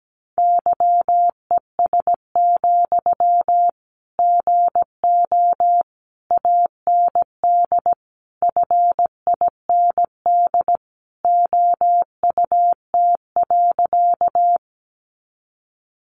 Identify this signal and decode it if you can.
Morse code, heard as a keyed tone, about 17 wpm, 700 Hz. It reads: YES, GO AND FIND OUT.
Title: Sentences from Top 500 Words 17wpm